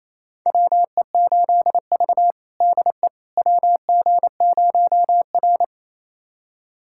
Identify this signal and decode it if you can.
Morse code, heard as a keyed tone, about 28 wpm, 700 Hz. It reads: WE8V DE WG0R
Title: USA Callsign DE Callsign - 4 Characters - 28wpm